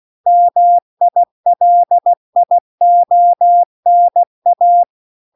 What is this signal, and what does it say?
Morse code, heard as a keyed tone, about 16 wpm, 700 Hz. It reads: MILIONA